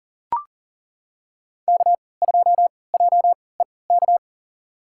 Morse code, heard as a keyed tone, about 38 wpm, spaced as for 22 wpm, 700 Hz. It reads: K2JEK
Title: USA Call Signs - 3x Character Spacing - 40wpm